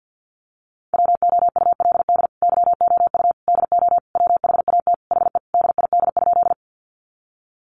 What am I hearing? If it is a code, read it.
Morse code, heard as a keyed tone, about 50 wpm, 700 Hz. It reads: WOULD YOU DO WHAT HE DID?
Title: Sentences from Top 100 Words 50wpm